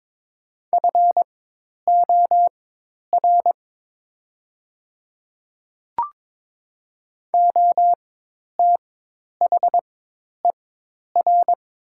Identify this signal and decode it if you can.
Morse code, heard as a keyed tone, about 22 wpm, spaced as for 10 wpm, 700 Hz. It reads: FOR OTHER
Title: 4x Character Spacing - Top 100 Words - 22wpm